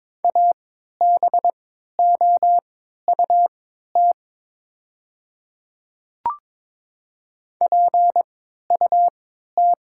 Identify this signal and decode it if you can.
Morse code, heard as a keyed tone, about 22 wpm, spaced as for 12 wpm, 700 Hz. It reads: ABOUT PUT
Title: 3x Character Spacing - Top 200 Words - 22wpm